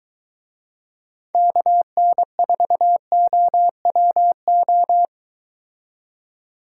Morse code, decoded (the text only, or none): KN4OWO